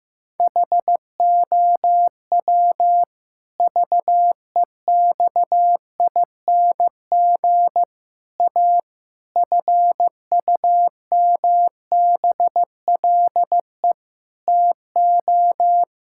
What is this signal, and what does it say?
Morse code, heard as a keyed tone, about 15 wpm, 700 Hz. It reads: HOW VEXING A FUMBLE TO
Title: Pangram Challenge v2 - 15wpm